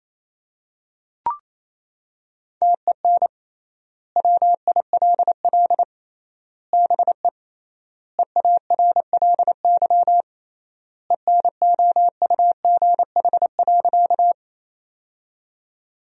Morse code, heard as a keyed tone, about 28 wpm, 700 Hz. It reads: TEN WILL BE EARLY ENOUGH.
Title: 3x Word-Spacing - Sentences from Top 400 Words - 28wpm